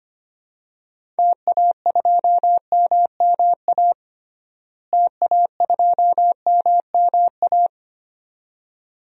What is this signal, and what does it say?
Morse code, heard as a keyed tone, about 25 wpm, 700 Hz. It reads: TA2MMA TA2MMA
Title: DXCC-100 Call Signs - 6 Characters - Repeated 2x - 25wpm